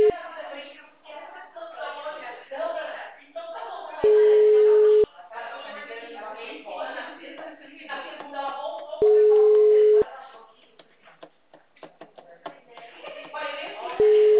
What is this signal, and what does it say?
Ringback tone (Europe, 425 Hz), with speech.
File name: out- -2006- - - .16950